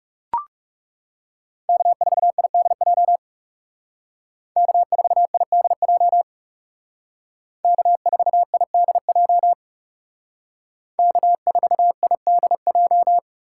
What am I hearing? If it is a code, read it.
Morse code, heard as a keyed tone, about 36 wpm, 700 Hz. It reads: K4IDJ K4IDJ K4IDJ K4IDJ
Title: Speed Racing - USA 1x3 Call Signs 30wpm